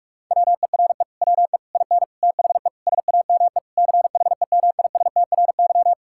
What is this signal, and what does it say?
Morse code, heard as a keyed tone, about 46 wpm, 700 Hz. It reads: WERE WE IN THE SAME CHEMISTRY
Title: Sentences from Top 1000 Words - Review Part 6 - 45wpm